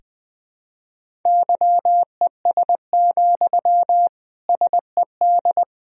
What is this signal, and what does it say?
Morse code, heard as a keyed tone, about 20 wpm, 700 Hz. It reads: YES, SED
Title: Sentences from Top 500 Words and Abbreviations 20wpm